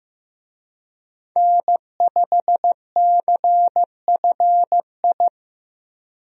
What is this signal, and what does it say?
Morse code, heard as a keyed tone, about 16 wpm, 700 Hz. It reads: N5CFI
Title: USA Call Signs - 5 Characters - Repeated 3x - 15wpm